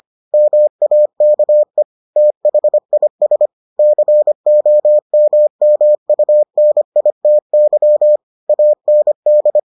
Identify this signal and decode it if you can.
Morse code, heard as a keyed tone, about 25 wpm, 600 Hz. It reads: MAKE THIS COMMUNITY AND